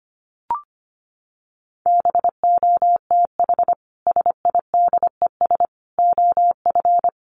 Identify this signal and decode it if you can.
Morse code, heard as a keyed tone, about 25 wpm, 700 Hz. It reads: BOTH SIDES OF
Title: Sets of 5 Words Encore2 25wpm